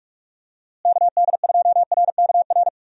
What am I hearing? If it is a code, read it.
Morse code, heard as a keyed tone, about 44 wpm, 700 Hz. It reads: KD2RKR
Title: USA Call Signs - 6 Characters - Repeated 2x - 45wpm